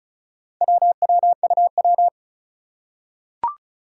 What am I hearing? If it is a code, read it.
Morse code, heard as a keyed tone, about 36 wpm, 700 Hz. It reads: WWUW
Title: Four Challenging Letter-Number Combinations 35wpm